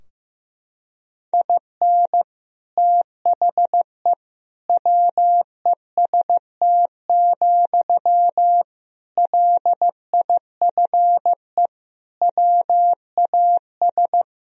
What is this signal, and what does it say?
Morse code, heard as a keyed tone, about 16 wpm, 700 Hz. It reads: IN THE WEST, LIFE WAS
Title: Sentences from Top 1000 Words - Review Part 2 - 15wpm